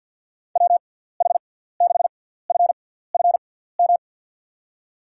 Morse code, heard as a keyed tone, about 53 wpm, spaced as for 17 wpm, 700 Hz. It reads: WH6FFM
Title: USA Call Signs - 6 Characters - 6x Character Spacing - 50wpm